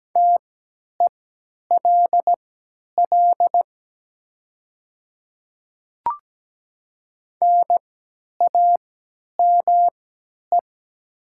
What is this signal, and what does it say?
Morse code, heard as a keyed tone, about 17 wpm, spaced as for 10 wpm, 700 Hz. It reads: TELL NAME
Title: 3x Character Spacing - Top 200 Words - 17wpm